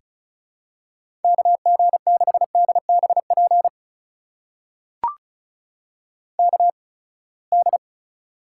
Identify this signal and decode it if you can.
Morse code, heard as a keyed tone, about 36 wpm, spaced as for 24 wpm, 700 Hz. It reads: KG6DBP KD
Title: USA Call Signs - 6 Characters - 8x Character Spacing - 35wpm